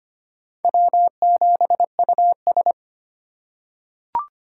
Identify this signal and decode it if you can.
Morse code, heard as a keyed tone, about 25 wpm, 700 Hz. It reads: W7US
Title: USA Call Signs - 4 Characters - 4x Character Spacing - 25wpm